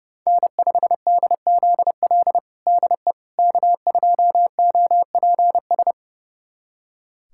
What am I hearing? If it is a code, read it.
Morse code, heard as a keyed tone, about 30 wpm, 700 Hz. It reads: N5DZL DE K2OPS
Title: USA Callsign DE Callsign - 5 Characters - 30wpm